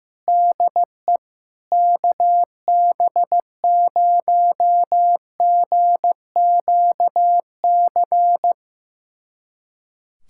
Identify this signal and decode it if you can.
Morse code, heard as a keyed tone, about 15 wpm, 700 Hz. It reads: DE KB0GQC
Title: USA Callsign DE Callsign - 6 Characters - 15wpm